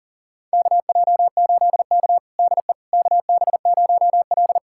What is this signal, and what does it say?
Morse code, heard as a keyed tone, about 40 wpm, 700 Hz. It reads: KJ8K DE KB0L
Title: USA Callsign DE Callsign - 4 Characters - 40wpm